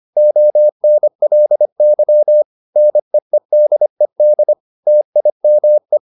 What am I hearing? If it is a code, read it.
Morse code, heard as a keyed tone, about 25 wpm, 600 Hz. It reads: ONLY NEEDED TIME
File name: the old man and the sea part 1 25wpm